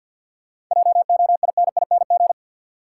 Morse code, heard as a keyed tone, about 51 wpm, 700 Hz. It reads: JOINING
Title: Sets of 5 Words - New Word Drill - 50wpm